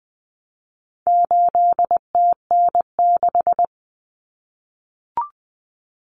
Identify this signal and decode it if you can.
Morse code, heard as a keyed tone, about 20 wpm, 700 Hz. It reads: 8TN6
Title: Common Cut Numbers - 4-Digit 20wpm